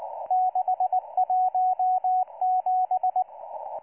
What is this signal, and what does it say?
Morse code, heard as a keyed tone, about 19 wpm, 740 Hz, with radio noise. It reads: 617